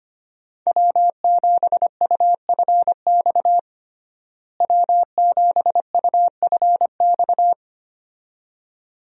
Morse code, heard as a keyed tone, about 25 wpm, 700 Hz. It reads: W7UFX W7UFX
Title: USA Call Signs - 5 Characters - Repeated 2x - 25wpm